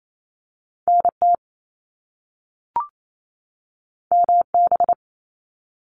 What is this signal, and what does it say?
Morse code, heard as a keyed tone, about 28 wpm, 700 Hz. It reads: NT MB
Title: Canadian Provinces and Territories 28wpm